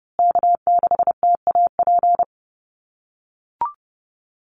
Morse code, heard as a keyed tone, about 30 wpm, 700 Hz. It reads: K6TAP
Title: USA Call Signs Repeated 2x 30wpm